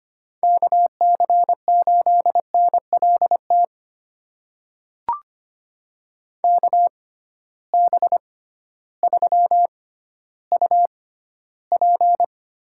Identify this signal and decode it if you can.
Morse code, heard as a keyed tone, about 25 wpm, spaced as for 14 wpm, 700 Hz. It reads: KC8NLT KB3UP
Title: USA Call Signs - 6 Characters - 6x Character Spacing - 25wpm